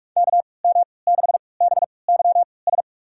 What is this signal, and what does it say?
Morse code, heard as a keyed tone, about 41 wpm, spaced as for 25 wpm, 700 Hz. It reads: KM6BYS
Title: USA Call Signs - 3x Character Spacing - 45wpm